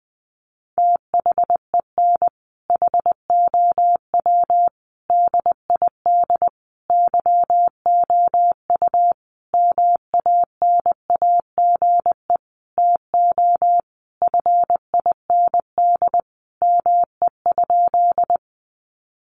Morse code, heard as a keyed tone, about 20 wpm, 700 Hz. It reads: THEN HOW DID YOU MANAGE TO FIND ME?
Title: Sentences from Top 1400 Words 20wpm